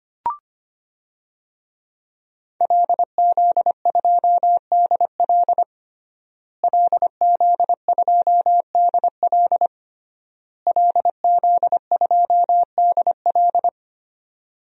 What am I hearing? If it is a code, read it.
Morse code, heard as a keyed tone, about 25 wpm, 700 Hz. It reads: LZ2DL LZ2DL LZ2DL
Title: DXCC-100 Call Signs - 5 Characters - Repeated 3x - 25wpm